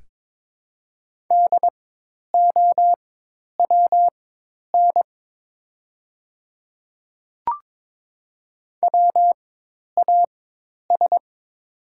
Morse code, heard as a keyed tone, about 22 wpm, spaced as for 10 wpm, 700 Hz. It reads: DOWN WAS
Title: 4x Character Spacing - Top 100 Words - 22wpm